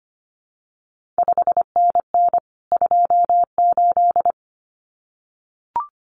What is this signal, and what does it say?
Morse code, heard as a keyed tone, about 25 wpm, 700 Hz. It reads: 5NN 28
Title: CQ WW DX Contest Exchange - 25wpm